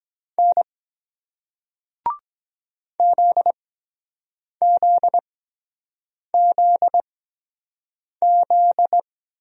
Morse code, heard as a keyed tone, about 24 wpm, spaced as for 7 wpm, 700 Hz. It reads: N ZZZZ